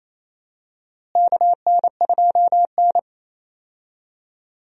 Morse code, heard as a keyed tone, about 28 wpm, 700 Hz. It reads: KN2N
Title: USA Call Signs - 4 Characters - Repeated 3x - 28wpm